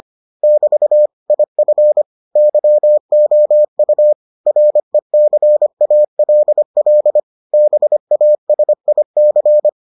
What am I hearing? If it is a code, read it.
Morse code, heard as a keyed tone, about 25 wpm, 600 Hz. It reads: <BT> IF YOU RECALL BASIC